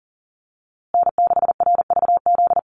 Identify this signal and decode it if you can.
Morse code, heard as a keyed tone, about 40 wpm, 700 Hz. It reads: N6RVZ
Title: USA Call Signs Repeated 3x 40wpm